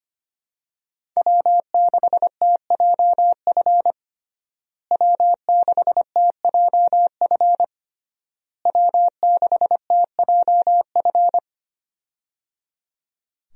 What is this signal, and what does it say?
Morse code, heard as a keyed tone, about 25 wpm, 700 Hz. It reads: W6TJF W6TJF W6TJF
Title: USA Call Signs - 5 Characters - Repeated 3x - 25wpm